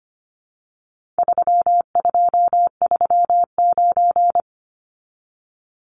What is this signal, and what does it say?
Morse code, heard as a keyed tone, about 25 wpm, 700 Hz. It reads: 3239
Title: Numbers - 4-Digit 25wpm